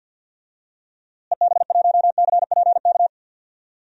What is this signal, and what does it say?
Morse code, heard as a keyed tone, about 53 wpm, 700 Hz. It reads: EB1CPK
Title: DXCC-100 Call Signs - 6 Characters - Repeated 3x - 50wpm